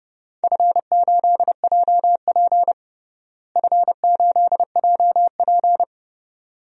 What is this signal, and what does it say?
Morse code, heard as a keyed tone, about 30 wpm, 700 Hz. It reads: F8JP F8JP
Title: DXCC-100 Call Signs - 4 Characters - Repeated 2x - 30wpm